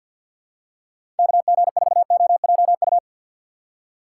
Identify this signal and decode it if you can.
Morse code, heard as a keyed tone, about 47 wpm, 700 Hz. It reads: KG4OJU
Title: USA Call Signs - 4x Character Spacing - 50wpm